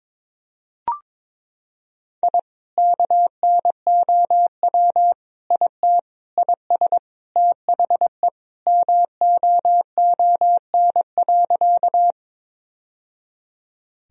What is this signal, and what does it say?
Morse code, heard as a keyed tone, about 22 wpm, 700 Hz. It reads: I KNOW IT IS THE MOON.
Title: Sentences from Top 1000 Words - Review Part 3 - 22wpm